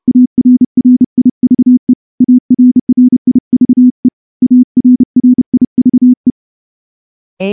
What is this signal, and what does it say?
Morse code, heard as a keyed tone, about 30 wpm, 250 Hz. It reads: ARRIVE ARRIVE ARRIVE
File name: morse-training-30WPM